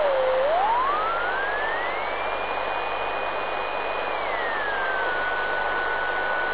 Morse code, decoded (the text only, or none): none